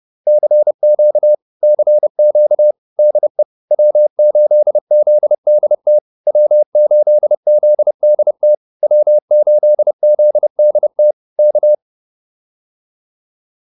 Morse code, heard as a keyed tone, about 30 wpm, 600 Hz. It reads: CQ CQ DE W8ZDT W8ZDT W8ZDT K
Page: advanced-practice-files